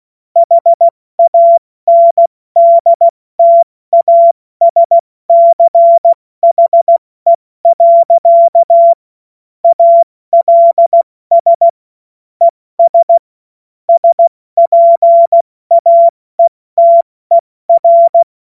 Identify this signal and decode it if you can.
Morse code, heard as a keyed tone, about 16 wpm, 680 Hz. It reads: HANDTASCHE. ALS ES SPAETER
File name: Morsekurs / Witze / Witz 08. November 2022 70 F80